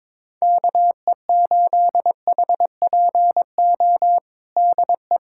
Morse code, heard as a keyed tone, about 21 wpm, 700 Hz. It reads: KE8HPO DE
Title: USA Callsign DE Callsign - 6 Characters - 22wpm